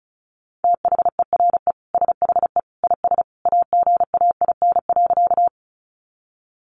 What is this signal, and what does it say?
Morse code, heard as a keyed tone, about 34 wpm, 700 Hz. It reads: THERE SHE IS AGAIN.
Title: Sentences from Top 200 Words 35wpm